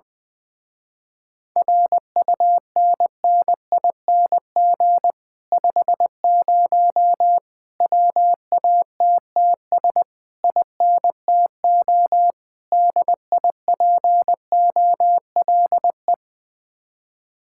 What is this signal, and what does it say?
Morse code, heard as a keyed tone, about 20 wpm, 700 Hz. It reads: RUNNING 50 WATTS INTO DIPOLE
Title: Speed-Racing Common QSO Phrases Encore - 17wpm